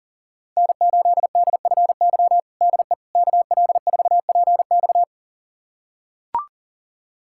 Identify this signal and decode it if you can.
Morse code, heard as a keyed tone, about 37 wpm, 700 Hz. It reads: N8DFY DE KL4PX
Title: USA Callsign DE Callsign - 5 Characters - 40wpm